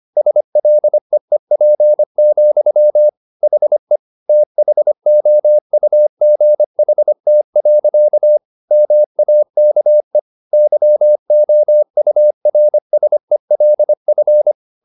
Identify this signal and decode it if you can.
Morse code, heard as a keyed tone, about 25 wpm, 600 Hz. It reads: SLEEP, HE THOUGHT. MAKE YOURSELF